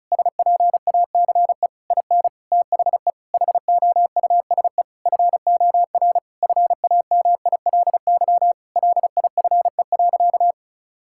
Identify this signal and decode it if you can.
Morse code, heard as a keyed tone, about 36 wpm, 700 Hz. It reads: SPACE IN THE HOUSE FOR FAMILY LIFE.
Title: Sentences from Top 1000 Words - Review Part 5 - 35wpm